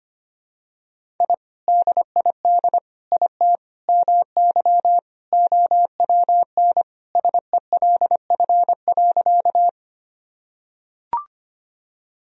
Morse code, heard as a keyed tone, about 25 wpm, 700 Hz. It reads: I DID IT MY OWN SELF.
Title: Sentences from Top 1000 Words - Review Part 1 - 25wpm